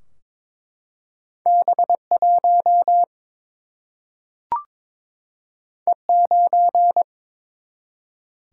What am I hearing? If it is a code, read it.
Morse code, heard as a keyed tone, about 22 wpm, 700 Hz. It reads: B 1 E 9
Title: Single-Letter Single-Number 22wpm